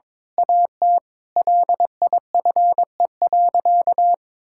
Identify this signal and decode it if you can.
Morse code, heard as a keyed tone, about 21 wpm, 700 Hz. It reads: AT LIFE.
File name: Difficult Sentences with Hints - 22wpm